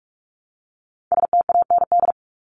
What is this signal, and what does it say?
Morse code, heard as a keyed tone, about 44 wpm, 700 Hz. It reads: STAND
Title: ING Suffix - Top 100 Words with Base Words 45wpm